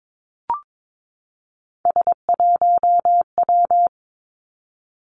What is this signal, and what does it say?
Morse code, heard as a keyed tone, about 22 wpm, 700 Hz. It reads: S1W
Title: Three Challenging Letter-Number Combinations 22wpm